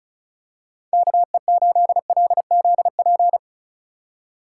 Morse code, heard as a keyed tone, about 36 wpm, 700 Hz. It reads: KE8LZP
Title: USA Call Signs - 6 Characters - 4x Character Spacing - 35wpm